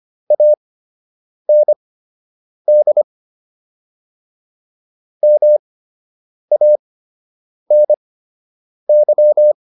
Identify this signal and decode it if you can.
Morse code, heard as a keyed tone, about 25 wpm, spaced as for 8 wpm, 600 Hz. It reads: AND MANY